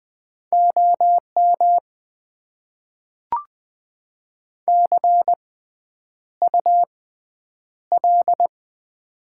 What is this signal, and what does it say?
Morse code, heard as a keyed tone, about 20 wpm, spaced as for 9 wpm, 700 Hz. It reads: OM CUL
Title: Top 90 QSO Elements - 6x Character Spacing - 20wpm